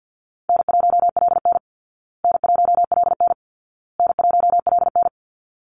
Morse code, heard as a keyed tone, about 50 wpm, 700 Hz. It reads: N1LN N1LN N1LN
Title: Call Signs Repeated 2x - 50wpm